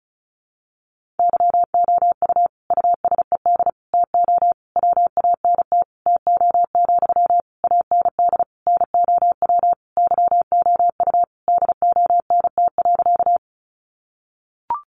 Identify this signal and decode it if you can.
Morse code, heard as a keyed tone, about 34 wpm, 700 Hz. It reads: YOU USED TO WANT TO, AND NOW YOU DONT.
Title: Sentences from Top 600 Words 35wpm